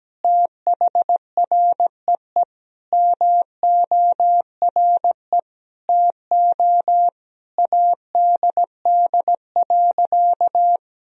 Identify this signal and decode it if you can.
Morse code, heard as a keyed tone, about 17 wpm, 700 Hz. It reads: THREE MORE TO ADD.
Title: Sentences from Top 1000 Words - Review Part 1 - 17wpm